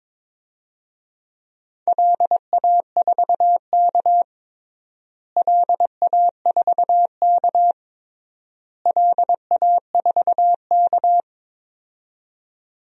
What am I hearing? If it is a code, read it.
Morse code, heard as a keyed tone, about 22 wpm, 700 Hz. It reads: LA4K LA4K LA4K
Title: DXCC-100 Call Signs - 4 Characters - Repeated 3x - 22wpm